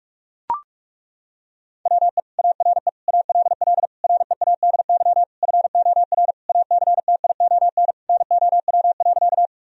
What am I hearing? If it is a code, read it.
Morse code, heard as a keyed tone, about 42 wpm, 700 Hz. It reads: WE ARE ALL READY FOR ACTION NOW.
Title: Sentences from Top 1000 Words - Review Part 4 - 45wpm